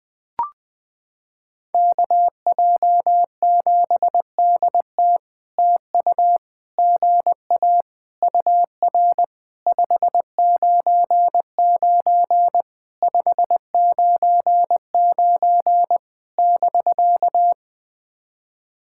Morse code, heard as a keyed tone, about 21 wpm, 700 Hz. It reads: KJ7DT TU GA UR 599 599 <BK>
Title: POTA QSO Practice - 20wpm